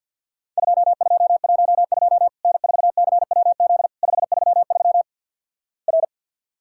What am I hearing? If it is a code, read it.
Morse code, heard as a keyed tone, about 50 wpm, 700 Hz. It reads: 2212 N4CWZ 533 R
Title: Worked All Europe DX Contest - QTC Exchange - 50wpm